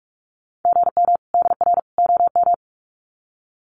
Morse code, heard as a keyed tone, about 44 wpm, 700 Hz. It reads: GM DR OM
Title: Common QSO Phrases 45wpm